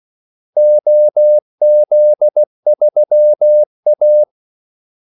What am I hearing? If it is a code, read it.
Morse code, heard as a keyed tone, about 16 wpm, 600 Hz. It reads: OZ3A
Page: Kaldesignaler 16 wpm